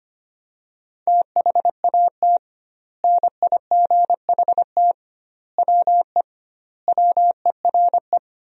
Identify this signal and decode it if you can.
Morse code, heard as a keyed tone, about 25 wpm, 700 Hz. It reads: THAT NIGHT WE WERE
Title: 2x Word-Spacing - Sentences from Top 300 Words - 25wpm